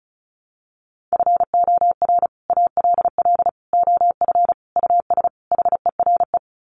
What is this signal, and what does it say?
Morse code, heard as a keyed tone, about 34 wpm, 700 Hz. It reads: FOR ALL OF US HERE
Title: Sets of 5 Words 35wpm